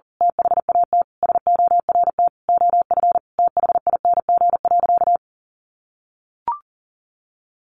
Morse code, heard as a keyed tone, about 40 wpm, 700 Hz. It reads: THAT SORT OF THING.
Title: Sentences from Top 1300 Words 40wpm